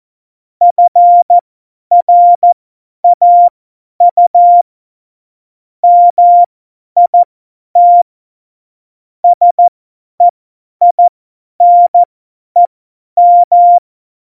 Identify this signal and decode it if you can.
Morse code, heard as a keyed tone, about 14 wpm, spaced as for 10 wpm, 700 Hz. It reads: FRAU MIT SEINEM
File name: Morsekurs / Witze / Witz 18 August 2020, 50 F70